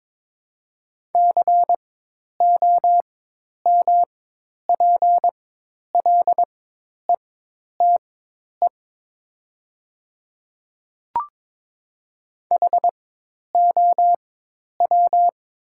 Morse code, heard as a keyed tone, about 21 wpm, spaced as for 10 wpm, 700 Hz. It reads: COMPLETE HOW
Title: 4x Character Spacing - Top 500 Words - 22wpm